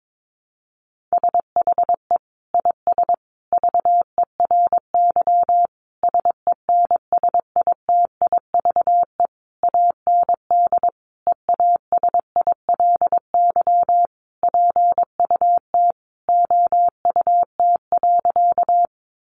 Morse code, heard as a keyed tone, about 22 wpm, 700 Hz. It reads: SHE IS VERY SENSITIVE AND EASILY PUT OUT.
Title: Sentences from Top 1400 Words 22wpm